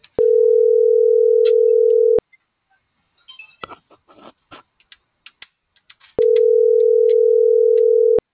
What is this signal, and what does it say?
Ringback tone (North America), 440 + 480 Hz.
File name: CENTRAL TERRESTRE / out cris.f